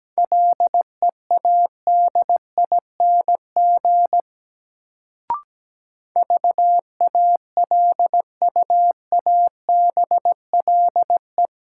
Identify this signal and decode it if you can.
Morse code, heard as a keyed tone, about 17 wpm, 700 Hz. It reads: LEADING VALUABLE
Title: Sentences from Top 1100 Words - New Word Drill - 17wpm